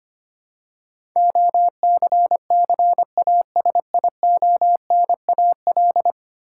Morse code, heard as a keyed tone, about 25 wpm, 700 Hz. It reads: OCCASIONAL
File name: 10 Letter Words - First 3 Letters Given - 25wpm